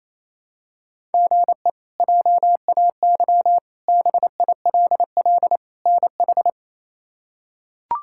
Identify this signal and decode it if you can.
Morse code, heard as a keyed tone, about 28 wpm, 700 Hz. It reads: GE JAY BILL NH
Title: K1USN SST QSO Practice - 28wpm